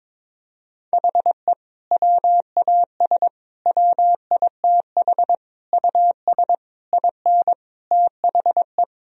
Morse code, heard as a keyed tone, about 22 wpm, 700 Hz. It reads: HE WAS WITH US IN THE
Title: Sentences from Top 1000 Words - Review Part 3 - 22wpm